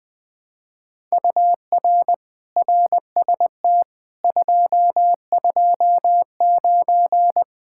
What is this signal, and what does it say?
Morse code, heard as a keyed tone, about 20 wpm, 700 Hz. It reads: UR RST 229
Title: Speed-Racing Common QSO Phrases - 20wpm